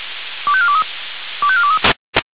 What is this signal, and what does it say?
Five-tone selective call (ZVEI), heard as quick runs of five tones, tones of about 70 ms. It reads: 25423 25423